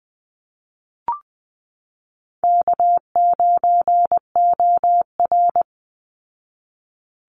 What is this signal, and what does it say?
Morse code, heard as a keyed tone, about 20 wpm, 700 Hz. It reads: K9OR
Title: Easy Call Signs Repeated 2x - 20wpm